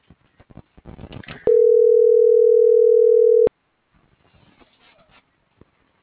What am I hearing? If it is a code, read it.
Ringback tone (North America), 440 + 480 Hz.